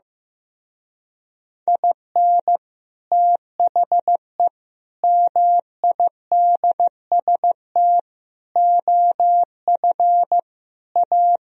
Morse code, heard as a keyed tone, about 15 wpm, 700 Hz. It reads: IN THE MIDST OF A
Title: Speed-Racing - Sets of 5 Words 15wpm